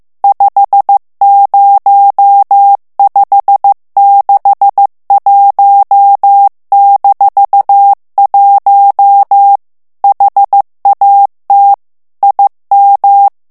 Morse code, heard as a keyed tone, about 15 wpm, 800 Hz. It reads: 50561-1 HAT IM
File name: ALLEMAND texte clair 1 vitesse 14 mots minute